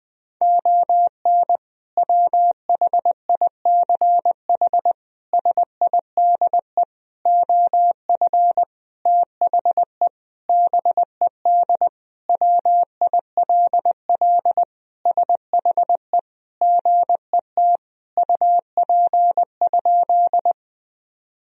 Morse code, heard as a keyed tone, about 20 wpm, 700 Hz. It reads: ON WHICH SIDE OF THE BED WILL SHE GET UP?
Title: Sentences from Top 1000 Words - Review Part 1 - 20wpm